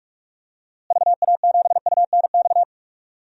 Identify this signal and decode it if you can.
Morse code, heard as a keyed tone, about 43 wpm, 700 Hz. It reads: VA7UNX
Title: DXCC-100 Call Signs - 6 Characters - Repeated 2x - 45wpm